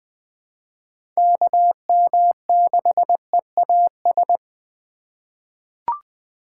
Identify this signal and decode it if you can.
Morse code, heard as a keyed tone, about 20 wpm, 700 Hz. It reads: KM6EAS
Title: USA Call Signs - 6 Characters - 4x Character Spacing - 20wpm